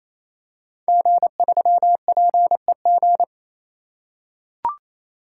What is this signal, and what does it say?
Morse code, heard as a keyed tone, about 28 wpm, 700 Hz. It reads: G3PEG
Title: Code Groups - 5 Letter:Numbers - 28wpm